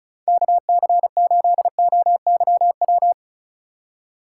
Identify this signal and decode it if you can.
Morse code, heard as a keyed tone, about 45 wpm, 700 Hz. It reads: KC8OYW